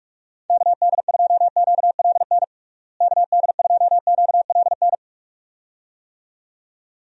Morse code, heard as a keyed tone, about 46 wpm, 700 Hz. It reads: KD2QLN KD2QLN
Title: USA Call Signs - 6 Characters - Repeated 2x - 45wpm